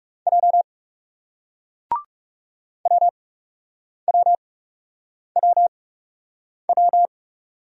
Morse code, heard as a keyed tone, about 40 wpm, 700 Hz. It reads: J W W W W